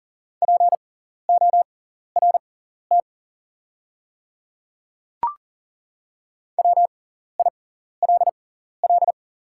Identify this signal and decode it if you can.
Morse code, heard as a keyed tone, about 41 wpm, spaced as for 14 wpm, 690 Hz. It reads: PORT WILL
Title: 6x Character Spacing - Top 300 Words - 40wpm